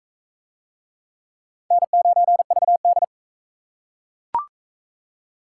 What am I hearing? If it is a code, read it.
Morse code, heard as a keyed tone, about 42 wpm, 700 Hz. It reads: N9VD